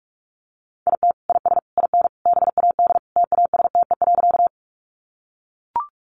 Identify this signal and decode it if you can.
Morse code, heard as a keyed tone, about 44 wpm, 700 Hz. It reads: IT IS IN BAD TASTE.
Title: Sentences from Top 1100 Words 45wpm